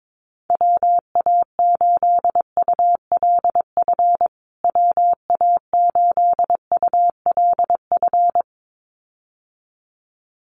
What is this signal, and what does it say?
Morse code, heard as a keyed tone, about 22 wpm, 700 Hz. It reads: WA8ULF WA8ULF
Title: USA Call Signs Repeated 2x 22wpm